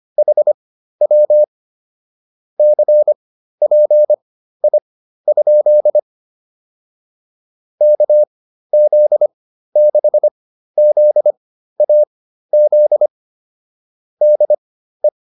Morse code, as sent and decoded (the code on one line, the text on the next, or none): .... .-- / -.-. .--. .. ..--.. / -.- --.. -.... --.. .- --.. / -.. .
HW CPI? KZ6ZAZ DE